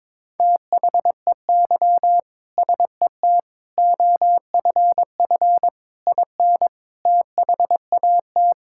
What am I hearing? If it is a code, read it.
Morse code, heard as a keyed tone, about 22 wpm, 700 Hz. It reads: THEY SET OFF IN THAT
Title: Sentences from Top 1000 Words - Review Part 5 - 22wpm